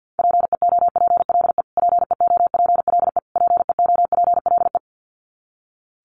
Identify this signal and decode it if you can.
Morse code, heard as a keyed tone, about 50 wpm, 700 Hz. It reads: PEOPLE PEOPLE PEOPLE
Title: Top 200 Words Repeated 50wpm